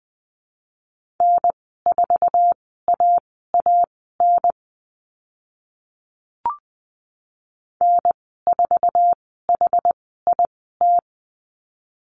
Morse code, heard as a keyed tone, about 20 wpm, spaced as for 14 wpm, 700 Hz. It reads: N4AAN N4HIT
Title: 069 Lesson 12g - Letter d - Call Signs 20wpm